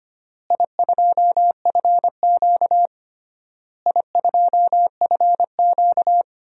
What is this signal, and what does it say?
Morse code, heard as a keyed tone, about 25 wpm, 700 Hz. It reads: I2FQ I2FQ
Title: DXCC-100 Call Signs - 4 Characters - Repeated 3x - 25wpm